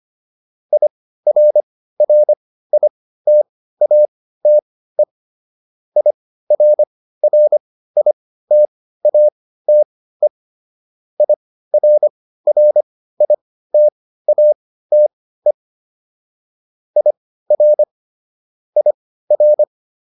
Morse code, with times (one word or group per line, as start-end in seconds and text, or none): 0.7-5.0 IRRITATE
6.0-10.3 IRRITATE
11.2-15.5 IRRITATE
17.0-17.8 IR
18.8-19.6 IR